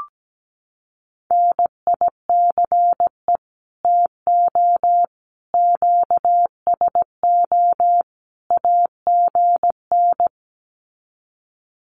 Morse code, heard as a keyed tone, about 17 wpm, 700 Hz. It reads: NICE TO QSO AGN
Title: Common QSO Phrases 17wpm